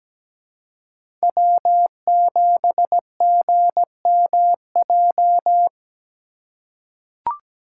Morse code, heard as a keyed tone, about 17 wpm, 700 Hz. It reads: W7GMJ
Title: USA Call Signs - 5 Characters - 3x Character Spacing - 17wpm